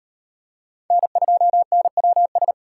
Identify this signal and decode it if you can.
Morse code, heard as a keyed tone, about 38 wpm, 700 Hz. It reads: N2NWS